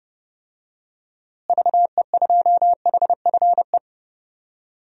Morse code, heard as a keyed tone, about 32 wpm, 700 Hz. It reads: VE2HFE
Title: DXCC-100 Call Signs - 6 Characters - Repeated 2x - 30wpm